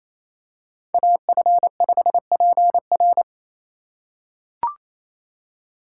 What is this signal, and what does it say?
Morse code, heard as a keyed tone, about 28 wpm, 700 Hz. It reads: AF5PR
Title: USA Call Signs - 5 Characters - 4x Character Spacing - 28wpm